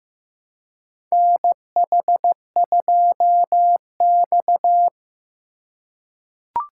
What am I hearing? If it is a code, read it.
Morse code, heard as a keyed tone, about 15 wpm, 700 Hz. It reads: NH2X
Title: USA Call Signs - 4 Characters - 4x Character Spacing - 15wpm